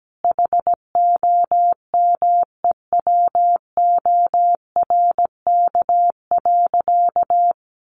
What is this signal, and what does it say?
Morse code, heard as a keyed tone, about 17 wpm, 700 Hz. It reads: HOMEWORK.
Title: Additional Easy Phrases 17wpm